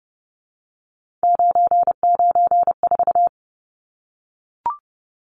Morse code, heard as a keyed tone, about 30 wpm, 700 Hz. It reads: 994
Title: Numbers - 3-Digit 30wpm